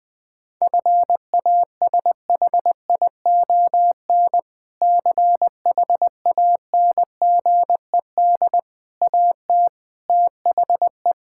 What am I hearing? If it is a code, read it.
Morse code, heard as a keyed tone, about 21 wpm, 700 Hz. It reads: FASHION CHANGED AT THE
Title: Sentences with 5-Word Phrases - Spoken Beginning - 20wpm